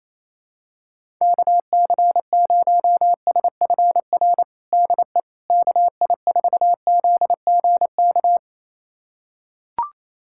Morse code, heard as a keyed tone, about 28 wpm, 700 Hz. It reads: KC0SFR DE KI4ZGK
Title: USA Callsign DE Callsign - 6 Characters - 28wpm